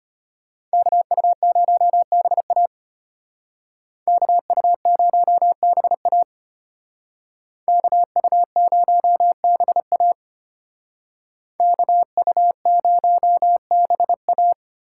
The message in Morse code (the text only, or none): KU0BA KU0BA KU0BA KU0BA